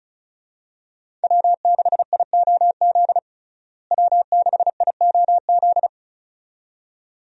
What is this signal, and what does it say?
Morse code, heard as a keyed tone, about 35 wpm, 700 Hz. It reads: W6IOZ W6IOZ
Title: USA Call Signs - 5 Characters - Repeated 2x - 35wpm